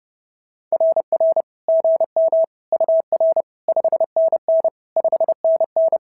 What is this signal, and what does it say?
Morse code, heard as a keyed tone, about 30 wpm, 690 Hz. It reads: RR GM UR 5NN 5NN
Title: SOTA QSO Practice 30wpm